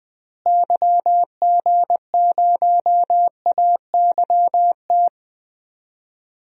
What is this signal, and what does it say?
Morse code, heard as a keyed tone, about 20 wpm, 700 Hz. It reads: YG0AYT
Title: DXCC-100 Call Signs - 6 Characters - Repeated 3x - 20wpm